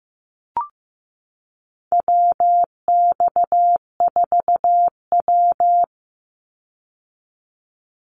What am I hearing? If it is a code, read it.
Morse code, heard as a keyed tone, about 15 wpm, 700 Hz. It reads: WX4W
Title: Call Signs Repeated 2x - 15wpm